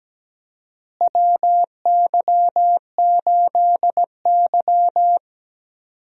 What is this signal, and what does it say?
Morse code, heard as a keyed tone, about 17 wpm, 700 Hz. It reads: WY8Y
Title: USA Call Signs - 4 Characters - Repeated 3x - 17wpm